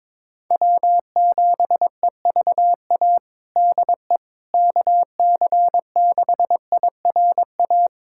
Morse code, heard as a keyed tone, about 22 wpm, 700 Hz. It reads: W7EVA DE KC6IRA
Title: USA Callsign DE Callsign 22wpm